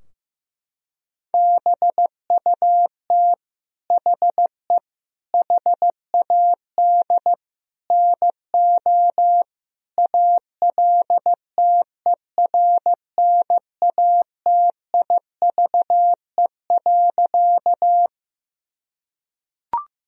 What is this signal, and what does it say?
Morse code, heard as a keyed tone, about 15 wpm, 700 Hz. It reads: BUT HE HAD NO ALTERNATIVE.
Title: Sentences from Top 1000 Words - Review Part 4 - 15wpm